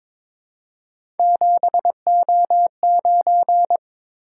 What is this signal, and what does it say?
Morse code, heard as a keyed tone, about 22 wpm, 700 Hz. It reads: 7O9
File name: practice / 22wpm / Three Letter-Number Combinations - 22wpm